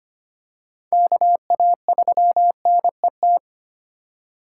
Morse code, heard as a keyed tone, about 25 wpm, 700 Hz. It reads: KA3NET
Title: USA Call Signs - 6 Characters - 4x Character Spacing - 25wpm